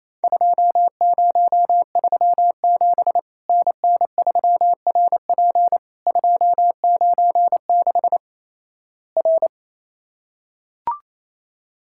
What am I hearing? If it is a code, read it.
Morse code, heard as a keyed tone, about 27 wpm, 700 Hz. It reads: 2037 NN3RP 296 R
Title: Worked All Europe DX Contest - QTC Exchange - 28wpm